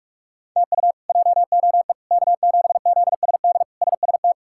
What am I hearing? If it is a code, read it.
Morse code, heard as a keyed tone, about 42 wpm, 700 Hz. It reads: TU JOE K7ZSD SST
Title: K1USN SST QSO Practice - 45wpm